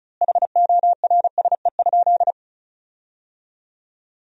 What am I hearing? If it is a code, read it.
Morse code, heard as a keyed tone, about 34 wpm, 700 Hz. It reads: HORSE?
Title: 2x Word-Spacing - Sentences from Top 500 Words - 35wpm